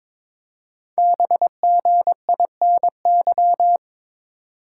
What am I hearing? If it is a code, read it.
Morse code, heard as a keyed tone, about 22 wpm, 700 Hz. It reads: BGINY
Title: Code Groups - 5 Letters - 22wpm